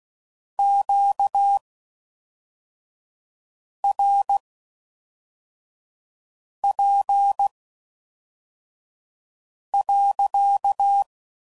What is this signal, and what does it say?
Morse code, heard as a keyed tone, about 16 wpm, spaced as for 4 wpm, 800 Hz. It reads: QRP.